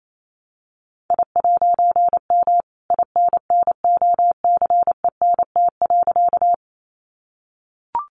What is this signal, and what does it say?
Morse code, heard as a keyed tone, about 28 wpm, 700 Hz. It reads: I'M INNOCENT.
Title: A Sound Of Thunder 28wpm